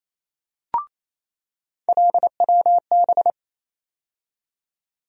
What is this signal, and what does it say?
Morse code, heard as a keyed tone, about 28 wpm, 700 Hz. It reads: LWB
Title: USA Call Sign Postfixes - 3 Letters 28wpm